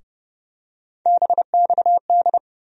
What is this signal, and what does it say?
Morse code, heard as a keyed tone, about 30 wpm, 700 Hz. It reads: BXD
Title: USA Call Sign Postfixes - 3 Letters 30wpm